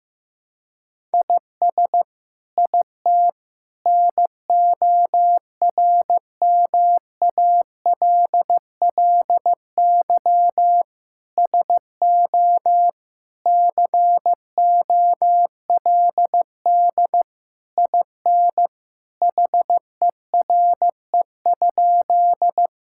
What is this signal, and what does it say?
Morse code, heard as a keyed tone, about 14 wpm, 700 Hz. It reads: IS IT NORMALLY SO COLD IN HERE?
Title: Sentences from Top 1000 Words - Review Part 6 - 15wpm